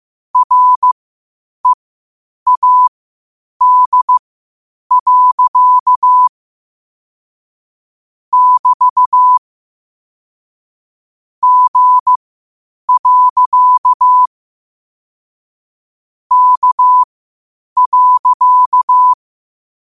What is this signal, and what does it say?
Morse code, heard as a keyed tone, about 15 wpm, spaced as for 8 wpm, 1000 Hz. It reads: READ. <BT> G. K.
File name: qotdfiles / 8wpm